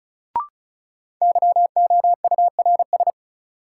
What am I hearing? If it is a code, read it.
Morse code, heard as a keyed tone, about 36 wpm, 700 Hz. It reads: YOURS
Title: Sentences from Top 1500 Words - New Word Drill - 35wpm